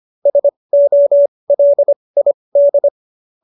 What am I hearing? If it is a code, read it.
Morse code, heard as a keyed tone, about 25 wpm, spaced as for 20 wpm, 590 Hz. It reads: SOLID
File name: QSO220 20